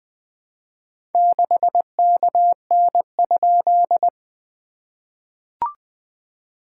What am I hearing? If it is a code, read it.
Morse code, heard as a keyed tone, about 21 wpm, 690 Hz. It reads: 6KN?
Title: SOTA QSO Practice 20wpm